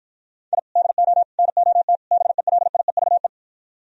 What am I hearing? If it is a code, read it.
Morse code, heard as a keyed tone, about 52 wpm, 700 Hz. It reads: I DO NOT BELIEVE